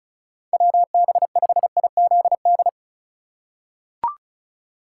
Morse code, heard as a keyed tone, about 36 wpm, 700 Hz. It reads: WB5IZD
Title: USA Call Signs - 6x Character Spacing - 35wpm